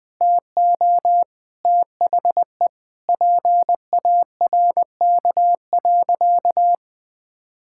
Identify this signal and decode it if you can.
Morse code, heard as a keyed tone, about 20 wpm, 700 Hz. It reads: TO THE PARK.
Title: Sentences with 4-Word Phrases - Spoken Beginning - 20wpm